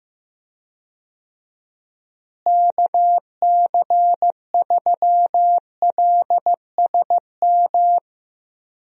Morse code, heard as a keyed tone, about 15 wpm, 700 Hz. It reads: KC3LSM
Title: USA Call Signs - 6 Characters - Repeated 2x - 15wpm